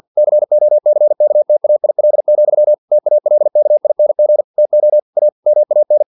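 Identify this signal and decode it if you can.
Morse code, heard as a keyed tone, about 50 wpm, 600 Hz. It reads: COCKTAIL, TALKING TO A MAN
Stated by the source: Morse practice text from an English-language novel at 49 wpm.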